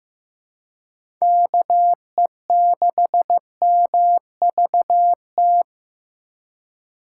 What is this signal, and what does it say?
Morse code, heard as a keyed tone, about 15 wpm, 700 Hz. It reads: KE6MVT
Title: USA Call Signs - 6 Characters - 2x Character Spacing - 15wpm